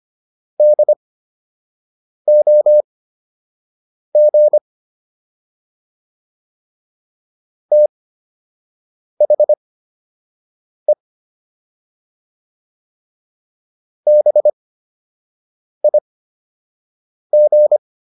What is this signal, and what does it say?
Morse code, heard as a keyed tone, about 25 wpm, spaced as for 6 wpm, 600 Hz. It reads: DOG THE BIG